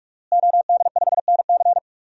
Morse code, heard as a keyed tone, about 43 wpm, 700 Hz. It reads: OD5NC
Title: DXCC-100 Call Signs - 5 Characters - Repeated 2x - 45wpm